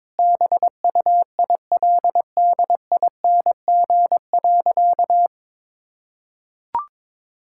Morse code, heard as a keyed tone, about 22 wpm, 700 Hz. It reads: BUILDING.
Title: Sentences from Top 1000 Words - Review Part 5 - 22wpm